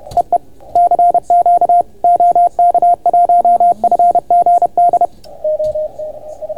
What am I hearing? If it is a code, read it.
Morse code, heard as a keyed tone, about 29 wpm, 690 Hz, with radio noise. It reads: EE CQ OK1FGD